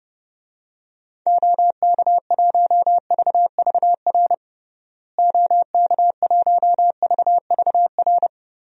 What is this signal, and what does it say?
Morse code, heard as a keyed tone, about 30 wpm, 700 Hz. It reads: OK1VVR OK1VVR
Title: DXCC-100 Call Signs - 6 Characters - Repeated 3x - 30wpm